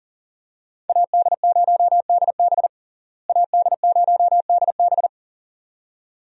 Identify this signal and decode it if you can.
Morse code, heard as a keyed tone, about 40 wpm, 700 Hz. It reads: AD0DB AD0DB
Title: USA Call Signs - 5 Characters - Repeated 2x - 40wpm